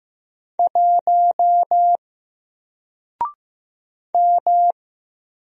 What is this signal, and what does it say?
Morse code, heard as a keyed tone, about 15 wpm, 700 Hz. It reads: 1 M
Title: Single Letter-Number Alt 15wpm